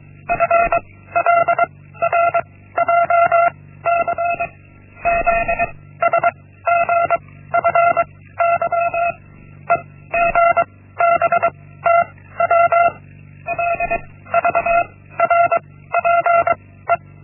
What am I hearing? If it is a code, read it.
Morse code, heard as a keyed tone, about 23 wpm, 680 Hz. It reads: F L R J C Z S G F Y E G B T W L V R P E
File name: Linksmakalnis-1